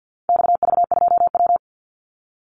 Morse code, heard as a keyed tone, about 48 wpm, 700 Hz. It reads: XV2W
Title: Call Signs Repeated 3x - 50wpm